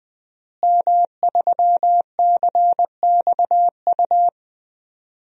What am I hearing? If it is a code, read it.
Morse code, heard as a keyed tone, about 20 wpm, 700 Hz. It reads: M3CXU
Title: DXCC-100 Call Signs - 5 Characters - Repeated 3x - 20wpm